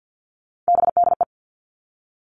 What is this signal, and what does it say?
Morse code, heard as a keyed tone, about 50 wpm, 700 Hz. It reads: BDE
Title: Three Challenging Letter-Number Combinations 50wpm